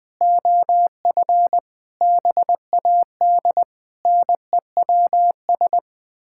Morse code, heard as a keyed tone, about 20 wpm, 700 Hz. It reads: OF BAD NEWS
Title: Sets of 9 Words - 20wpm